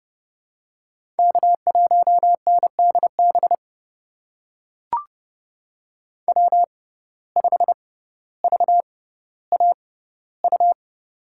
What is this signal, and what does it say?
Morse code, heard as a keyed tone, about 32 wpm, spaced as for 16 wpm, 700 Hz. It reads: K1NDB W5VAU
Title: USA Call Signs - 6x Character Spacing - 30wpm